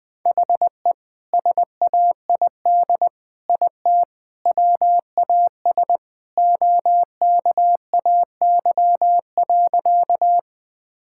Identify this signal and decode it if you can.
Morse code, heard as a keyed tone, about 21 wpm, 700 Hz. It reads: HE SAID IT WAS OKAY.
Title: Sentences with 2-Word Phrases - 20wpm